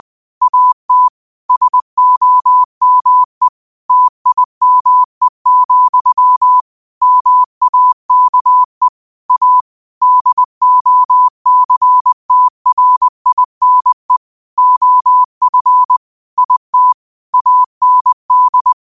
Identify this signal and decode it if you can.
Morse code, heard as a keyed tone, about 20 wpm, 1000 Hz. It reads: AT SOME TIME, MAKE A DOCTRINE OF IT AND